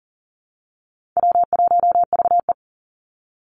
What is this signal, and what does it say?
Morse code, heard as a keyed tone, about 40 wpm, 700 Hz. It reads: W1VE
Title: Easy Call Signs - 40wpm